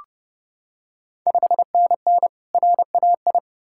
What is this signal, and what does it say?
Morse code, heard as a keyed tone, about 32 wpm, 700 Hz. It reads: 5NN RAI
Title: Ontario QSO Party Exchange 30wpm